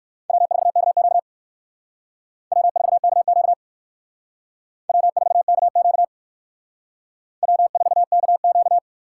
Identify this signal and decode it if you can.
Morse code, heard as a keyed tone, about 56 wpm, 700 Hz. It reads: W4KQ W4KQ W4KQ W4KQ
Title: Speed Racing - USA 1x2 Call Signs 45wpm